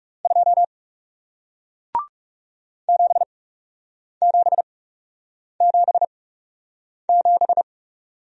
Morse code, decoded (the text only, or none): 2 7777